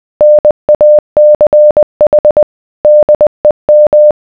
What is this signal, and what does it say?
Morse code, heard as a keyed tone, about 20 wpm, 600 Hz. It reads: NACH DEM